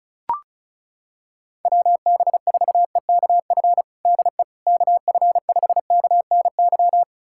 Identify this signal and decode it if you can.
Morse code, heard as a keyed tone, about 34 wpm, 700 Hz. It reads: WB4EKF DE KF5KNY
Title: USA Callsign DE Callsign - 6 Characters - 35wpm